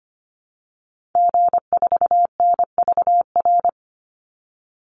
Morse code, heard as a keyed tone, about 25 wpm, 700 Hz. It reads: G4NVR
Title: Call Signs Repeated 3x - 25wpm